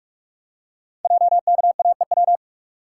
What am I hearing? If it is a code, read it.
Morse code, heard as a keyed tone, about 44 wpm, 700 Hz. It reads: JKAEW
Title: Code Groups - 5 Letters - 45wpm